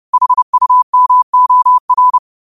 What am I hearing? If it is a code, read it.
Morse code, heard as a keyed tone, about 30 wpm, 1000 Hz. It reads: HUMOR
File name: qotdfiles / 30wpm